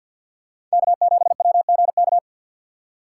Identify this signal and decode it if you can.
Morse code, heard as a keyed tone, about 48 wpm, 700 Hz. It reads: K7WGK
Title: USA Call Signs - 5 Characters - 2x Character Spacing - 50wpm